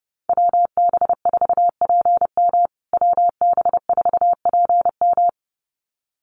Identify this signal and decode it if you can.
Morse code, heard as a keyed tone, about 30 wpm, 700 Hz. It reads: WB4PM WB4PM
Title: USA Call Signs Repeated 2x 30wpm